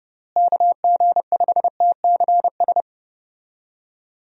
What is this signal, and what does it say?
Morse code, heard as a keyed tone, about 30 wpm, 700 Hz. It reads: KG5TCS
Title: USA Call Signs - 6 Characters - Repeated 2x - 30wpm